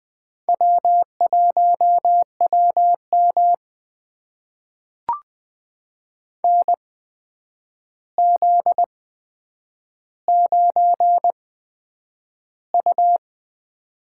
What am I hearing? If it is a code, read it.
Morse code, heard as a keyed tone, about 20 wpm, spaced as for 9 wpm, 700 Hz. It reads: W1WM NZ9U
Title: USA Call Signs - 4 Characters - 8x Character Spacing - 20wpm